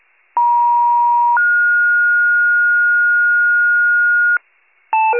Two-tone page, heard as a one-second tone, then a three-second tone, about 950 then 1430 Hz, with noise. It is heard from 0.4 to 4.4 s.